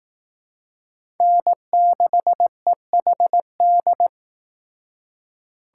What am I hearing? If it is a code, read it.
Morse code, heard as a keyed tone, about 19 wpm, 700 Hz. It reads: N6EHD
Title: Speed Racing - USA Call Signs 15wpm